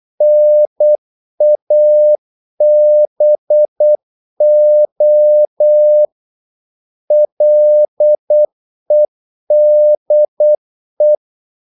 Morse code, heard as a keyed tone, about 8 wpm, 600 Hz. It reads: NABO LEDE
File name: Korte ord DK 8wpm